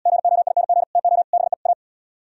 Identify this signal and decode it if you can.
Morse code, heard as a keyed tone, about 78 wpm, 700 Hz. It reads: GOING TO BE A